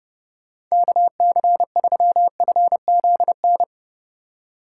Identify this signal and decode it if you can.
Morse code, heard as a keyed tone, about 37 wpm, 700 Hz. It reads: KC3FZN
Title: Speed Racing - USA 2x3 Call Signs 30wpm